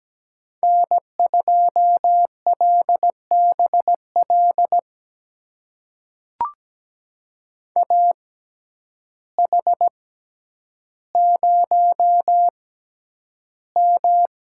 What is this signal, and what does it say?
Morse code, heard as a keyed tone, about 17 wpm, spaced as for 9 wpm, 700 Hz. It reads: N2LBL AH0M
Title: USA Call Signs - 6x Character Spacing - 17wpm